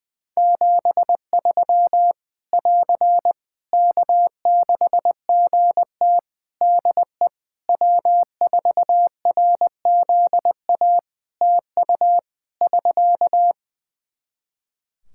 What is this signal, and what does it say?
Morse code, heard as a keyed tone, about 21 wpm, 700 Hz. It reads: 73 <AR> K6GT DE W4RZA TU <SK>
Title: Practice QSO - Exchange 4 - Ending - 20wpm